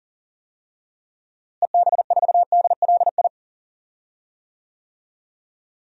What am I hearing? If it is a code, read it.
Morse code, heard as a keyed tone, about 40 wpm, 700 Hz. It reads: EB4DLI
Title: DXCC-100 Call Signs - 6 Characters - Repeated 2x - 40wpm